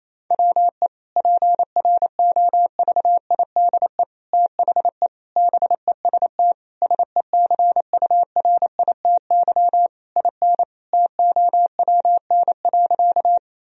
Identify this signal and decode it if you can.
Morse code, heard as a keyed tone, about 28 wpm, 700 Hz. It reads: WE PROVIDE THE BEST SECURITY IN TOWN.
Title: Sentences from Top 1000 Words - Review Part 4 - 28wpm